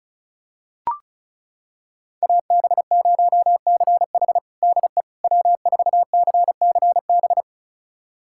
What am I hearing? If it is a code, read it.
Morse code, heard as a keyed tone, about 34 wpm, 700 Hz. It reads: AB0CH DE W4CCB
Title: USA Callsign DE Callsign - 5 Characters - 35wpm